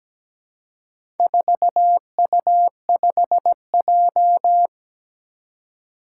Morse code, heard as a keyed tone, about 17 wpm, 700 Hz. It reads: 4U5J
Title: Four Challenging Letter-Number Combinations 17wpm